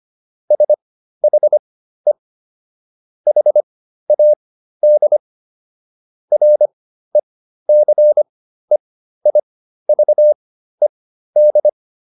Morse code, heard as a keyed tone, about 25 wpm, spaced as for 13 wpm, 600 Hz. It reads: SHE HAD RECEIVED